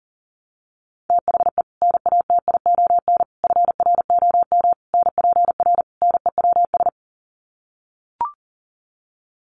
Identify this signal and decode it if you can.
Morse code, heard as a keyed tone, about 40 wpm, 700 Hz. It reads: THE NATION FROM NPR NEWS
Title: Sets of 5 Words 40wpm